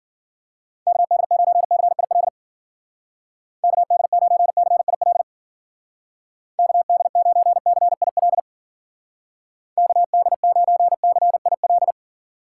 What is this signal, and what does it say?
Morse code, heard as a keyed tone, about 51 wpm, 700 Hz. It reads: KD9CIL KD9CIL KD9CIL KD9CIL